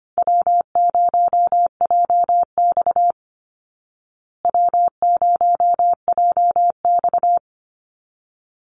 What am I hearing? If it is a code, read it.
Morse code, heard as a keyed tone, about 25 wpm, 700 Hz. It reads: W0JX W0JX
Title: Call Signs Repeated 3x - 25wpm